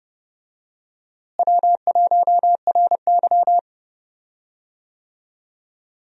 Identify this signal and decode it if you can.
Morse code, heard as a keyed tone, about 30 wpm, 700 Hz. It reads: W1RY
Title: USA Call Signs - 4 Characters - Repeated 3x - 30wpm